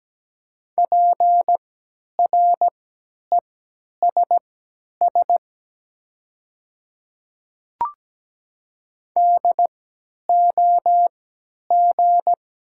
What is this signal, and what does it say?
Morse code, heard as a keyed tone, about 17 wpm, spaced as for 10 wpm, 700 Hz. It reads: PRESS DOG
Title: 3x Character Spacing - Top 500 Words - 17wpm